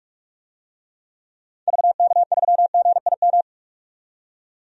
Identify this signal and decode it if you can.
Morse code, heard as a keyed tone, about 48 wpm, 700 Hz. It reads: VK3GIM
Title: DXCC-100 Call Signs - 6 Characters - 45wpm